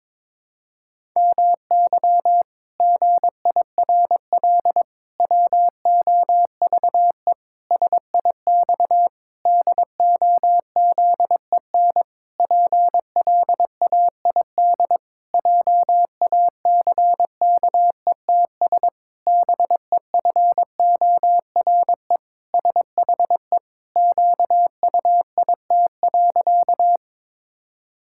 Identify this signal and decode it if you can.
Morse code, heard as a keyed tone, about 22 wpm, 700 Hz. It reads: MY GIRL WOVE SIX DOZEN PLAID JACKETS BEFORE SHE QUIT.
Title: Pangram Challenge v2 - 22wpm